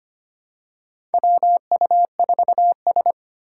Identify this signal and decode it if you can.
Morse code, heard as a keyed tone, about 25 wpm, 700 Hz. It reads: WU4S
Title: USA Call Signs - 4 Characters - 2x Character Spacing - 25wpm